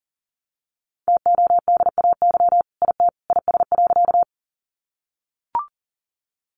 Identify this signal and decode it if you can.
Morse code, heard as a keyed tone, about 40 wpm, 700 Hz. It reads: TODAY IT IS.
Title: Sentences from Top 600 Words 40wpm